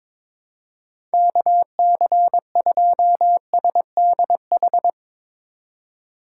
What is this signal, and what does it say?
Morse code, heard as a keyed tone, about 22 wpm, 700 Hz. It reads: KC2SDH
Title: USA Call Signs - 6 Characters - 4x Character Spacing - 22wpm